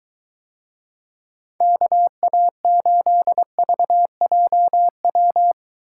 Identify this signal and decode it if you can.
Morse code, heard as a keyed tone, about 24 wpm, 700 Hz. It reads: KA8VJW